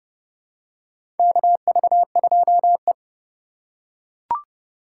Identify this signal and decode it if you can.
Morse code, heard as a keyed tone, about 32 wpm, 700 Hz. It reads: KV2E
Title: USA Call Signs - 4 Characters - 2x Character Spacing - 30wpm